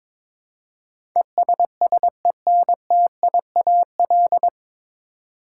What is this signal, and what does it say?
Morse code, heard as a keyed tone, about 22 wpm, 700 Hz. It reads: ESSENTIAL
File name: 9 Letter Words - First 3 Letters Given - 22wpm